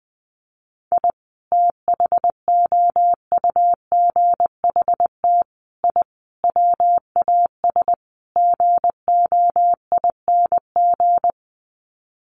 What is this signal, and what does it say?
Morse code, heard as a keyed tone, about 20 wpm, 700 Hz. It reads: I THOUGHT I WAS GOING
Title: Sets of 5 Words Encore2 20wpm